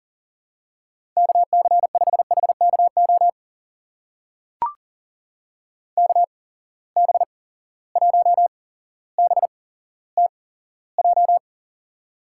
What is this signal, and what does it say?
Morse code, heard as a keyed tone, about 41 wpm, spaced as for 17 wpm, 700 Hz. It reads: KC5HKO KB1BTJ
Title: USA Call Signs - 6 Characters - 8x Character Spacing - 40wpm